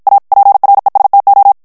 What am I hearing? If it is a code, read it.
Morse code, heard as a keyed tone, about 54 wpm, 800 Hz. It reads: A PRESTO
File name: ITALIEN texte clair 1 vitesse 50 mots minute